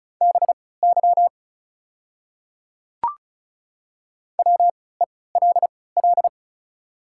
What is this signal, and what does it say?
Morse code, heard as a keyed tone, about 36 wpm, spaced as for 20 wpm, 700 Hz. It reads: BY WELL
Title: 3x Character Spacing - Top 300 Words - 35wpm